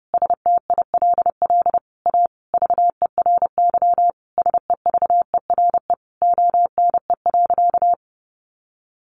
Morse code, heard as a keyed tone, about 30 wpm, 700 Hz. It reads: STILL A VERY SEVERE ONE.
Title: Sentences from Top 1400 Words 30wpm